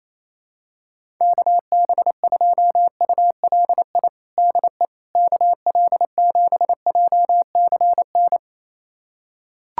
Morse code, heard as a keyed tone, about 28 wpm, 700 Hz. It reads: KB2ULI DE KL7JCN
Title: USA Callsign DE Callsign 28wpm